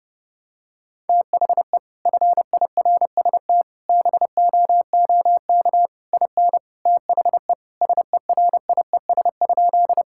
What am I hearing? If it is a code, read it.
Morse code, heard as a keyed tone, about 30 wpm, 700 Hz. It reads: THE FIRST BOOK IN THE SERIES?
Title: Sentences from Top 1000 Words - Review Part 5 - 30wpm